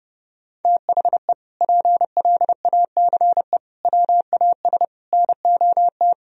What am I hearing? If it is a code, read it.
Morse code, heard as a keyed tone, about 30 wpm, 700 Hz. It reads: THE PLACE WAS NOT
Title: Sentences from Top 1000 Words - Review Part 1 - 30wpm